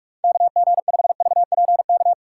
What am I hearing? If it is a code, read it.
Morse code, heard as a keyed tone, about 46 wpm, 700 Hz. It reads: KG5VPK
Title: USA Call Signs - 6 Characters - 6x Character Spacing - 45wpm